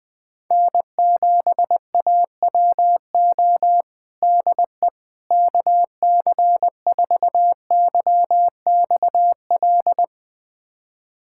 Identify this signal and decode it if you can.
Morse code, heard as a keyed tone, about 20 wpm, 700 Hz. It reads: N7AWO DE KC4YXL
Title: USA Callsign DE Callsign 20wpm